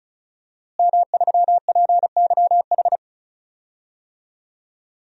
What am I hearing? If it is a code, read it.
Morse code, heard as a keyed tone, about 36 wpm, 700 Hz. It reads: M3PYH
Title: DXCC-100 Call Signs - 5 Characters - Repeated 3x - 35wpm